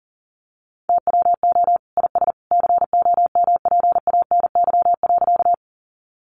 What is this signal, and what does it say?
Morse code, heard as a keyed tone, about 40 wpm, 700 Hz. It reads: TWO IS COMPANY.
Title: Sentences from Top 600 Words 40wpm